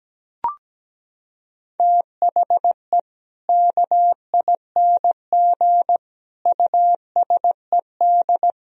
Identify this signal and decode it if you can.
Morse code, heard as a keyed tone, about 17 wpm, 700 Hz. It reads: THE KING USED
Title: Sentences from Top 1000 Words - Review Part 4 - 17wpm